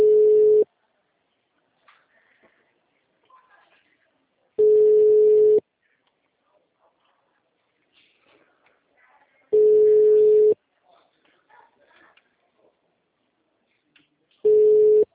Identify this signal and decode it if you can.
Ringback tone (Europe, 425 Hz).
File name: out- -2006- - - .85